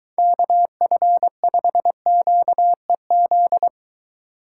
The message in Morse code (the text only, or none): KF5QEZ